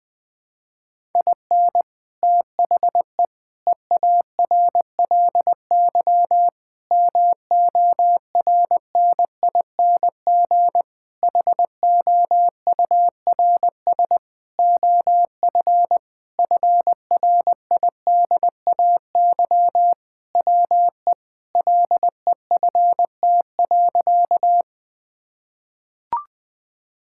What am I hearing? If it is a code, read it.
Morse code, heard as a keyed tone, about 21 wpm, 700 Hz. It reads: IN THE EARLY MORNING HOURS OF FRIDAY WE LEFT.
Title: Sentences with 6-Word Phrases - 20wpm